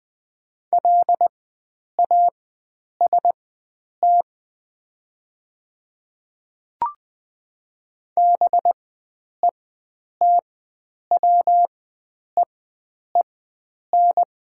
Morse code, heard as a keyed tone, about 20 wpm, spaced as for 9 wpm, 700 Hz. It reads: LAST BETWEEN
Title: 4x Character Spacing - Top 300 Words - 20wpm